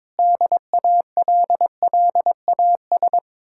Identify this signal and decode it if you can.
Morse code, heard as a keyed tone, about 22 wpm, 700 Hz. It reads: DALLAS
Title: Speed-Racing Common QSO Phrases - 22wpm